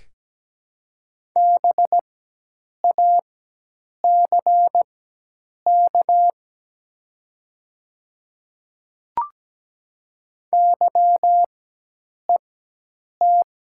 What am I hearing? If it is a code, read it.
Morse code, heard as a keyed tone, about 17 wpm, spaced as for 8 wpm, 700 Hz. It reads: BACK YET
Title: 4x Character Spacing - Top 500 Words - 17wpm